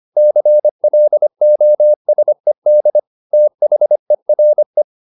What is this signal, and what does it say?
Morse code, heard as a keyed tone, about 25 wpm, 600 Hz. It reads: CLOSED THERE
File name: the old man and the sea part 1 25wpm